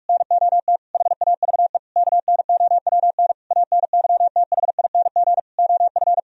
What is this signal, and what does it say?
Morse code, heard as a keyed tone, about 46 wpm, 700 Hz. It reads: NOT HAVE KNOWN ANYTHING OF
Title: Sentences from Top 1000 Words - Review Part 3 - 45wpm